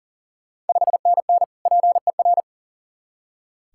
Morse code, heard as a keyed tone, about 40 wpm, 690 Hz. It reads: 5NN PER
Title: Ontario QSO Party Exchange 40wpm